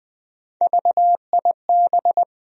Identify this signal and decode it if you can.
Morse code, heard as a keyed tone, about 20 wpm, 700 Hz. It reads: VIB
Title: USA Call Sign Postfixes - 3 Letters 20wpm